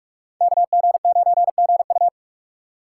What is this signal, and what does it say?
Morse code, heard as a keyed tone, about 44 wpm, 700 Hz. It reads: KG9GU